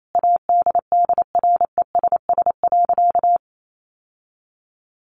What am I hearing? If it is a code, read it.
Morse code, heard as a keyed tone, about 28 wpm, 700 Hz. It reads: ADDRESS.
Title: Sentences from Top 1100 Words 28wpm